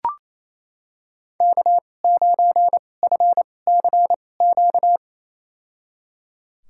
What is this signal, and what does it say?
Morse code, heard as a keyed tone, about 28 wpm, spaced as for 20 wpm, 700 Hz. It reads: K9FCQ
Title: USA Call Signs - 5 Characters - 2x Character Spacing - 28wpm